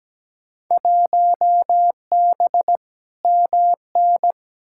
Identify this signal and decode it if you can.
Morse code, heard as a keyed tone, about 17 wpm, 700 Hz. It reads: 1B MN
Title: Field Day Exchanges - 17wpm